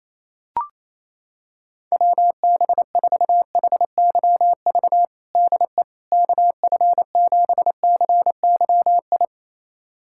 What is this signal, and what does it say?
Morse code, heard as a keyed tone, about 28 wpm, 700 Hz. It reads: WB4HYV DE KF7CYI
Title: USA Callsign DE Callsign - 6 Characters - 28wpm